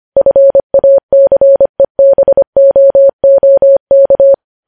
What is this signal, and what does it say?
Morse code, heard as a keyed tone, about 25 wpm, 550 Hz. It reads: FACEBOOK
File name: podcast / News headlines-1x-25x25